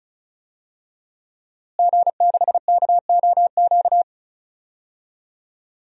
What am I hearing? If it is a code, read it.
Morse code, heard as a keyed tone, about 34 wpm, 700 Hz. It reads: G6KOQ
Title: DXCC-100 Call Signs - 5 Characters - Repeated 2x - 35wpm